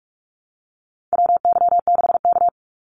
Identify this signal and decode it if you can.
Morse code, heard as a keyed tone, about 44 wpm, 700 Hz. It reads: WY6K
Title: Call Signs - 45wpm